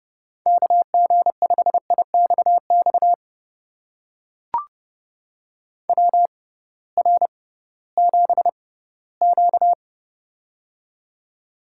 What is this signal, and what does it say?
Morse code, heard as a keyed tone, about 30 wpm, spaced as for 17 wpm, 700 Hz. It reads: KG5IXX WR7Q
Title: USA Call Signs - 6x Character Spacing - 30wpm